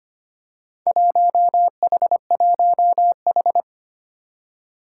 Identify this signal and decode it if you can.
Morse code, heard as a keyed tone, about 25 wpm, 700 Hz. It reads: 1H1H
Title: Four Challenging Letter-Number Combinations 25wpm